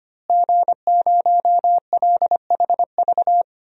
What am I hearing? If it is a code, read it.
Morse code, heard as a keyed tone, about 25 wpm, 700 Hz. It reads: G0LHV
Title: DXCC-100 Call Signs - 5 Characters - Repeated 3x - 25wpm